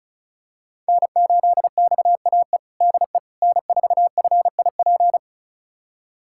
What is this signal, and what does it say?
Morse code, heard as a keyed tone, about 36 wpm, 700 Hz. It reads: N8XAE DE N4FIP
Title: USA Callsign DE Callsign - 5 Characters - 35wpm